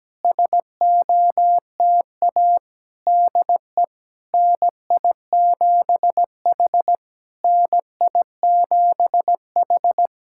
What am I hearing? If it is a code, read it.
Morse code, heard as a keyed tone, about 17 wpm, 700 Hz. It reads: SOTA DE NI7H NI7H
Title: SOTA QSO Practice 17wpm